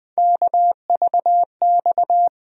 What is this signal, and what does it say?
Morse code, heard as a keyed tone, about 20 wpm, 700 Hz. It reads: KVX
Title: USA Call Sign Postfixes - 3 Letters 20wpm